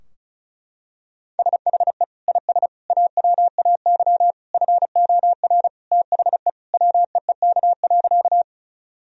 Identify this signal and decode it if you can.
Morse code, heard as a keyed tone, about 34 wpm, 700 Hz. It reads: SHE IS AWAY FOR THE WEEK.
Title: Sentences from Top 1000 Words - Review Part 2 - 35wpm